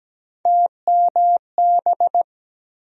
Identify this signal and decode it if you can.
Morse code, heard as a keyed tone, about 17 wpm, 700 Hz. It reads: TMB
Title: USA Call Sign Postfixes - 3 Letters 17wpm